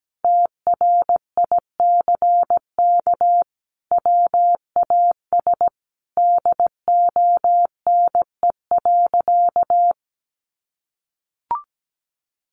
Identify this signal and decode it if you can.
Morse code, heard as a keyed tone, about 17 wpm, 700 Hz. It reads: TRICK WAS DONE.
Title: Sentences from Top 1500 Words 17wpm